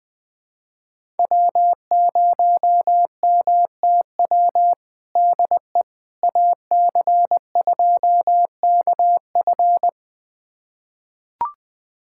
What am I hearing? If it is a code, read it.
Morse code, heard as a keyed tone, about 21 wpm, 700 Hz. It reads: W0MTW DE AC2KF
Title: USA Callsign DE Callsign - 5 Characters - 20wpm